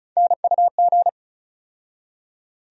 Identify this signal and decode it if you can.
Morse code, heard as a keyed tone, about 34 wpm, 700 Hz. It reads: NUG
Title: USA Call Sign Postfixes - 3 Letters 35wpm